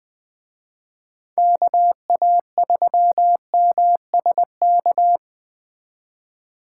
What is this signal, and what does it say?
Morse code, heard as a keyed tone, about 20 wpm, 700 Hz. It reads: KA3MSK
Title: Speed Racing - USA 2x3 Call Signs 17wpm